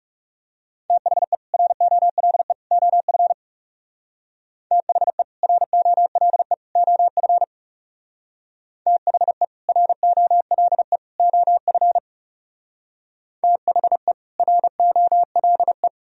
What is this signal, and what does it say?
Morse code, heard as a keyed tone, about 37 wpm, 700 Hz. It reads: THE ROLE OF THE ROLE OF THE ROLE OF THE ROLE
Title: Speed-Racing - Sets of 3 Words 30wpm